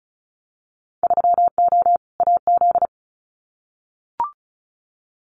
Morse code, heard as a keyed tone, about 34 wpm, 700 Hz. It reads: 3O AZ
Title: Winter Field Day Exchanges 35wpm